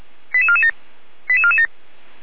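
Five-tone selective call (ZVEI), heard as quick runs of five tones, tones of about 80 ms. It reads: 88447 88447
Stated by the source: ZVEI five-tone call 89250 received on an SDR, from the Feuerwehr Orlamünde audio server.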